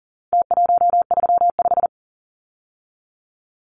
Morse code, heard as a keyed tone, about 40 wpm, 700 Hz. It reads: T135
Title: Numbers - Contest Style - 4-Digit 40wpm